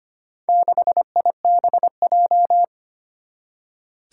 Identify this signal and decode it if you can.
Morse code, heard as a keyed tone, about 25 wpm, 700 Hz. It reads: 6IBJ
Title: Four Challenging Letter-Number Combinations 25wpm